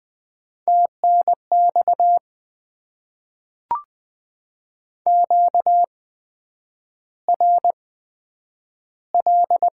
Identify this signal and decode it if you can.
Morse code, heard as a keyed tone, about 20 wpm, spaced as for 9 wpm, 700 Hz. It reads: TNX QRL
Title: Top 20 QSO Elements - 8x Character Spacing - 20wpm